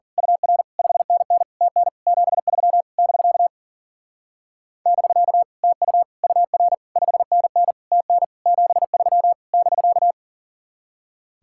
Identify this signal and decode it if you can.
Morse code, heard as a keyed tone, about 41 wpm, 700 Hz. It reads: UR 5NN TN 73 <BK> <BK> TU UR 5NN TN 73 <BK>
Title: Speed-Racing Common QSO Phrases Encore - 40wpm